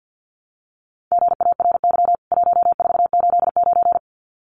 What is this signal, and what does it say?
Morse code, heard as a keyed tone, about 50 wpm, 700 Hz. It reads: GARY 1489
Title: CWT Exchange 50wpm